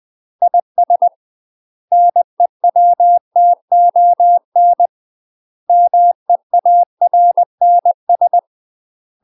Morse code, heard as a keyed tone, about 20 wpm, 700 Hz. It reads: IS NEWTON MEARNS
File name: UK MASTER QSOs TRAINING / 20WPM / UK MASTER QSOs 0003